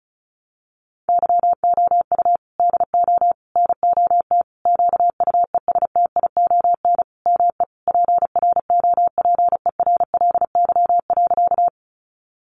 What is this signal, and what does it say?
Morse code, heard as a keyed tone, about 34 wpm, 700 Hz. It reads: YOU DO NOT QUESTION ME PROPERLY.
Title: Sentences from Top 1100 Words 35wpm